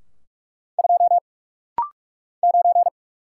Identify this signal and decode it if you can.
Morse code, heard as a keyed tone, about 44 wpm, 700 Hz. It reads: 2 9
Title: Single Letter-Number - Rapid-Fire 45wpm